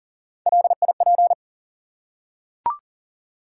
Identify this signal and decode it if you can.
Morse code, heard as a keyed tone, about 40 wpm, 700 Hz. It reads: LIP
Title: Three Letter Words - 40wpm